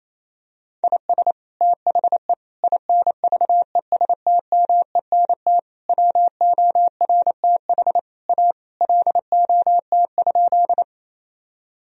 Morse code, heard as a keyed tone, about 28 wpm, 700 Hz. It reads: IS THE INVESTMENT WORTH A LOT?
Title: Sentences from Top 1000 Words - Review Part 4 - 28wpm